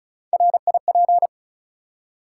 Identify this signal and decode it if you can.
Morse code, heard as a keyed tone, about 36 wpm, 700 Hz. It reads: RIP
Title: Three Letter Words - 35wpm